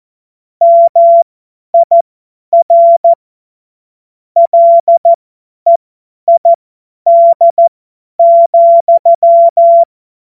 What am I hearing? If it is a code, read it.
Morse code, heard as a keyed tone, about 14 wpm, spaced as for 10 wpm, 680 Hz. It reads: MIR LEID,
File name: Morsekurs / Witze / Witz 17. Januar 2023 50 F70